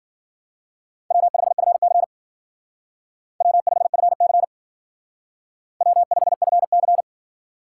Morse code, heard as a keyed tone, about 48 wpm, 700 Hz. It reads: W5FC W5FC W5FC
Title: Speed Racing - USA 1x2 Call Signs 40wpm